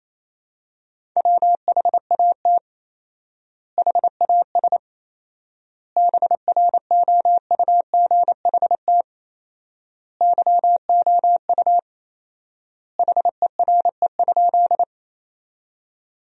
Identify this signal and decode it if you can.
Morse code, heard as a keyed tone, about 28 wpm, 700 Hz. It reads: WHAT HAS BROUGHT YOU HERE?
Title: 4x Word-Spacing - Sentences from Top 500 Words - 28wpm